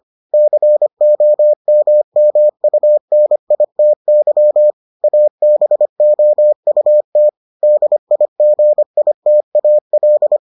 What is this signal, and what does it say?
Morse code, heard as a keyed tone, about 25 wpm, 600 Hz. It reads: COMMUNITY ABOUT DIGITAL